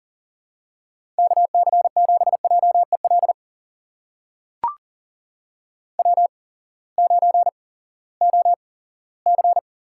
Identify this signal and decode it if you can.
Morse code, heard as a keyed tone, about 40 wpm, spaced as for 20 wpm, 700 Hz. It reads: KC7JEL W9OC
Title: USA Call Signs - 8x Character Spacing - 40wpm